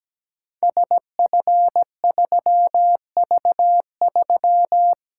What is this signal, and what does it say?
Morse code, heard as a keyed tone, about 17 wpm, 700 Hz. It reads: SF3V3
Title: Code Groups - 5 Letter:Numbers - 17wpm